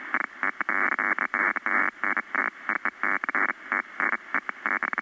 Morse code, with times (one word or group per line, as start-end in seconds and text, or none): none